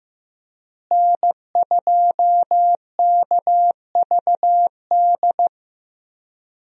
Morse code, heard as a keyed tone, about 15 wpm, 700 Hz. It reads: N2KVD
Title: USA Call Signs - 5 Characters - Repeated 2x - 15wpm